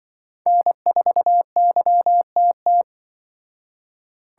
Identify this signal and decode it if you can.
Morse code, heard as a keyed tone, about 23 wpm, 700 Hz. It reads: N4YTT
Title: Speed Racing - USA 1x3 Call Signs 20wpm